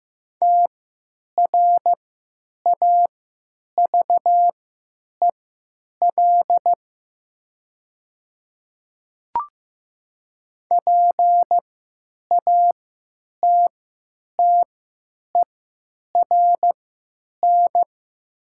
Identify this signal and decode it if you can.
Morse code, heard as a keyed tone, about 15 wpm, spaced as for 9 wpm, 700 Hz. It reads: TRAVEL PATTERN
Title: 3x Character Spacing - Top 500 Words - 15wpm